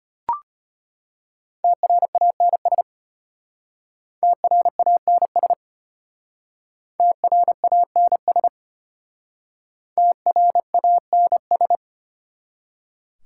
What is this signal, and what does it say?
Morse code, heard as a keyed tone, about 33 wpm, 700 Hz. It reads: TRANS TRANS TRANS TRANS